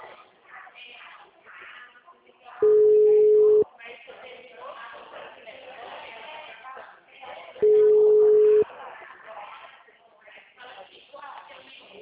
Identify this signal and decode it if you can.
Ringback tone (Europe, 425 Hz), with speech.
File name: out- -2029- - - .18915